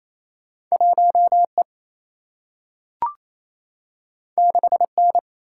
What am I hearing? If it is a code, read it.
Morse code, heard as a keyed tone, about 28 wpm, 700 Hz. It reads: 1 E 6 N
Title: Single-Letter Single-Number 28wpm